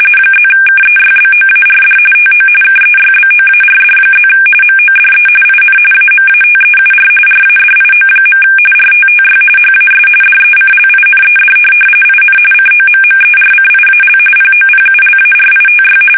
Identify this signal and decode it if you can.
RTTY, about 45.45 baud. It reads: +KEYBOARD TO KEYBOARD+ MODE IN AMATEUR RADIO. RTTY HAS DECLINED IN COMMERCIAL POPULARITY
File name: fsk850 / teletype-RTTY